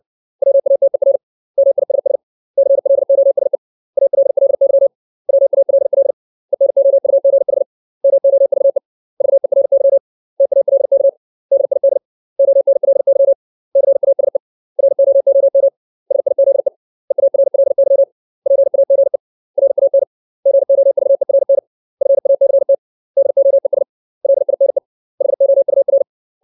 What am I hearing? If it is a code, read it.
Morse code, heard as a keyed tone, about 60 wpm, 550 Hz. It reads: WATER MISS CLOSE ALLY WALL ENOUGH MOVE VERY TALK DID ONLY CASE ROOM SIZE EARLY PAGE RAN GOVERN FACT DOES LINE HOUR
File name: 632 words / 632 Words 60WPM